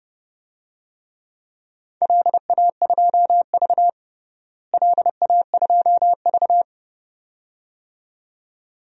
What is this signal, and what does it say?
Morse code, heard as a keyed tone, about 30 wpm, 700 Hz. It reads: LA2V LA2V
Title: DXCC-100 Call Signs - 4 Characters - Repeated 2x - 30wpm